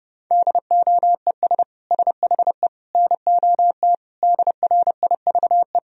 Morse code, heard as a keyed tone, about 30 wpm, 700 Hz. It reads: DOES SHE NOT DRIVE
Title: Sentences from Top 1000 Words - Review Part 2 - 30wpm